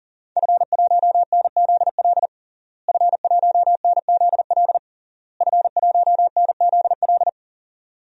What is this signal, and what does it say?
Morse code, heard as a keyed tone, about 40 wpm, 700 Hz. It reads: F1NZL F1NZL F1NZL
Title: DXCC-100 Call Signs - 5 Characters - Repeated 3x - 40wpm